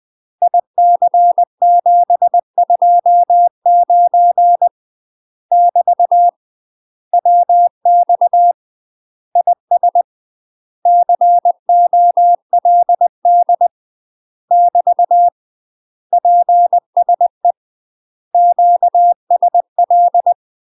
Morse code, heard as a keyed tone, about 20 wpm, 700 Hz. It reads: IC729 <BT> WX IS COLD <BT> PSE QSL
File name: UK MASTER QSOs TRAINING/20WPM